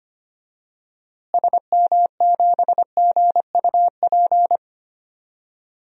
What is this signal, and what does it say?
Morse code, heard as a keyed tone, about 25 wpm, 700 Hz. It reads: SM7GUP
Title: DXCC-100 Call Signs - 6 Characters - Repeated 2x - 25wpm